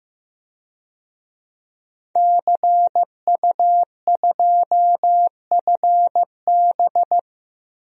Morse code, heard as a keyed tone, about 15 wpm, 700 Hz. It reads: CU2FB
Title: DXCC-100 Call Signs - 5 Characters - 15wpm